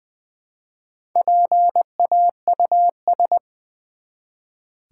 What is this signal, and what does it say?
Morse code, heard as a keyed tone, about 20 wpm, 700 Hz. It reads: PAUS
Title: Build A Word - 1 Letter At A Time - 5-Letter Words - 20wpm